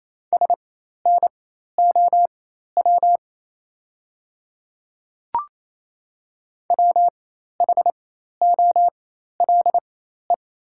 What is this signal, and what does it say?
Morse code, heard as a keyed tone, about 28 wpm, spaced as for 13 wpm, 700 Hz. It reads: SNOW WHOLE
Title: 4x Character Spacing - Top 500 Words - 28wpm